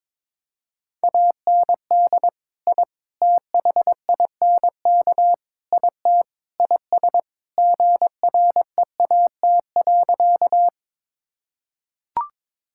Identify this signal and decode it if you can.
Morse code, heard as a keyed tone, about 21 wpm, 700 Hz. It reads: AND I THINK IT IS GREAT.
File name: practice / 22wpm / Sentences with 4-Word Phrases - 22wpm